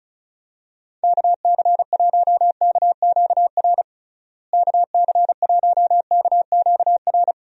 Morse code, heard as a keyed tone, about 36 wpm, 700 Hz. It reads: KC1KQR KC1KQR
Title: USA Call Signs - 6 Characters - Repeated 2x - 35wpm